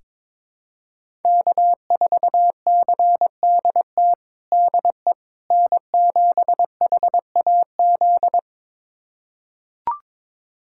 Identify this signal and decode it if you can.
Morse code, heard as a keyed tone, about 22 wpm, 700 Hz. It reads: K4CDT DE N7HAZ
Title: USA Callsign DE Callsign - 5 Characters - 22wpm